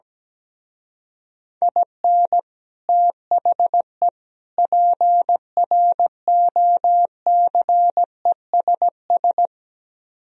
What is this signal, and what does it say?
Morse code, heard as a keyed tone, about 17 wpm, 700 Hz. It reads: IN THE PROCESS
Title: Speed-Racing - Sets of 3 Words 17wpm